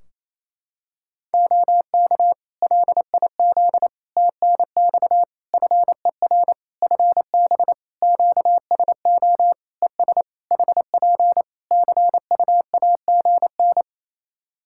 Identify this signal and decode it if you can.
Morse code, heard as a keyed tone, about 28 wpm, 700 Hz. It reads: OK LIZ TNX FER FB QSO ES HP CUAGN
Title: Practice QSO - Exchange 4 - Ending - 28wpm